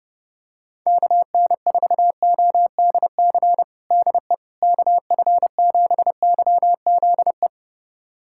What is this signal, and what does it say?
Morse code, heard as a keyed tone, about 32 wpm, 700 Hz. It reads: KN4ODC DE KF7YZE
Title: USA Callsign DE Callsign 30wpm